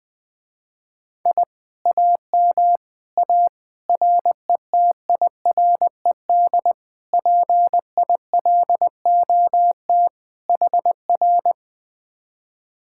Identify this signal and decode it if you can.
Morse code, heard as a keyed tone, about 20 wpm, 700 Hz. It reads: I AM A RETIRED PILOT HR
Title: Speed-Racing Common QSO Phrases - 20wpm